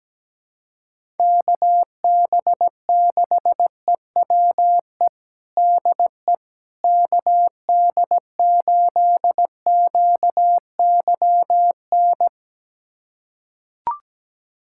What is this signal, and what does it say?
Morse code, heard as a keyed tone, about 17 wpm, 700 Hz. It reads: KB6EWE DE KD8QYN
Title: USA Callsign DE Callsign - 6 Characters - 17wpm